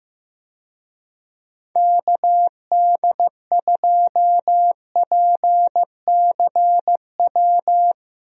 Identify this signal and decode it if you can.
Morse code, heard as a keyed tone, about 16 wpm, 700 Hz. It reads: KD2PCW
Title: USA Call Signs - 6 Characters - Repeated 2x - 15wpm